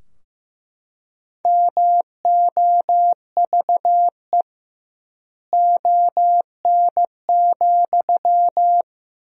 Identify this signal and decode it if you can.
Morse code, heard as a keyed tone, about 15 wpm, 700 Hz. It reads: MOVE ON,
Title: 2x Word-Spacing - Sentences from Top 500 Words - 15wpm